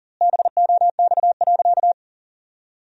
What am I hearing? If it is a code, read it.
Morse code, heard as a keyed tone, about 40 wpm, 700 Hz. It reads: BOX.
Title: 2x Word-Spacing - Sentences from Top 500 Words - 40wpm